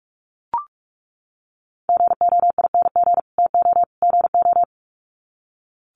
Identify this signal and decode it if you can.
Morse code, heard as a keyed tone, about 44 wpm, 700 Hz. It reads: GOING TO GO
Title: Sets of 3 Words 45wpm